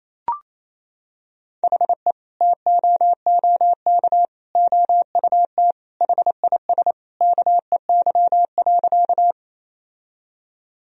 Morse code, heard as a keyed tone, about 28 wpm, 700 Hz. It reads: HE TOOK OUT HIS KEY.
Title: Sentences from Top 1000 Words - Review Part 4 - 28wpm